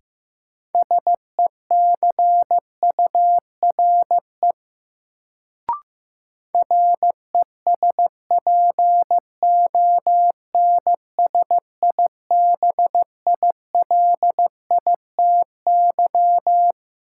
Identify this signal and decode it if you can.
Morse code, heard as a keyed tone, about 15 wpm, 700 Hz. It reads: SECURE RESPONSIBILITY
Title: Sentences from Top 1200 Words - New Word Drill - 15wpm